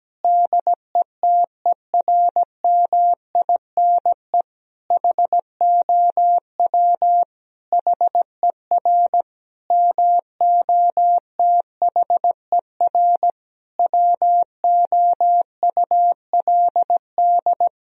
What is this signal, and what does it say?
Morse code, heard as a keyed tone, about 17 wpm, 700 Hz. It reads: DETERMINE HOW HER MOTHER WOULD
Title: Sentences from Top 1000 Words - Review Part 4 - 17wpm